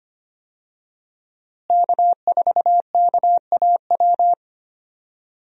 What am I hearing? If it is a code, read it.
Morse code, heard as a keyed tone, about 25 wpm, 700 Hz. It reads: K4KAW
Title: USA Call Signs - 5 Characters - Repeated 2x - 25wpm